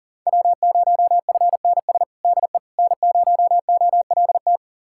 Morse code, heard as a keyed tone, about 40 wpm, 690 Hz. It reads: W0FNS DE N0OLT
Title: USA Callsign DE Callsign - 5 Characters - 40wpm